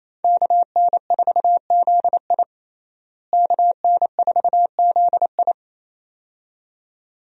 Morse code, heard as a keyed tone, about 28 wpm, 700 Hz. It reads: KN4ZI KN4ZI
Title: USA Call Signs - 5 Characters - Repeated 2x - 28wpm